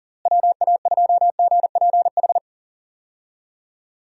Morse code, heard as a keyed tone, about 40 wpm, 700 Hz. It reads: WA2GPH
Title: USA Call Signs - 6 Characters - Repeated 3x - 40wpm